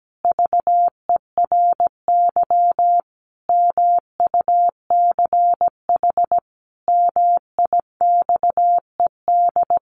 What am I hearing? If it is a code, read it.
Morse code, heard as a keyed tone, about 17 wpm, 700 Hz. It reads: VERY MUCH MIXED
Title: Sentences from Top 1500 Words 17wpm